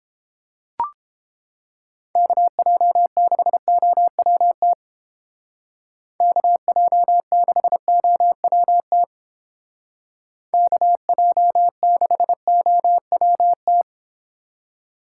Morse code, heard as a keyed tone, about 28 wpm, 700 Hz. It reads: KJ6OWT KJ6OWT KJ6OWT